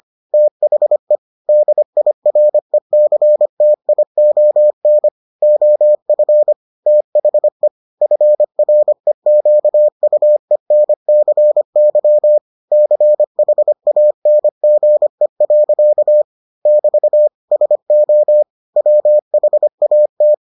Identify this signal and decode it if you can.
Morse code, heard as a keyed tone, about 25 wpm, 600 Hz. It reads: THE DIRECTION OF THE FREQUENCY CHANGE. <BT> SO WHAT